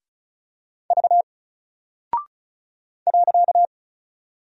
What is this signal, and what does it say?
Morse code, heard as a keyed tone, about 36 wpm, 700 Hz. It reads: V .
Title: Single Character (Letter:Number:Punctuation:Prosign) Alt 35wpm